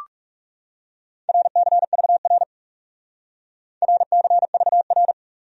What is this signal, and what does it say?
Morse code, heard as a keyed tone, about 38 wpm, 700 Hz. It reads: RCVR RCVR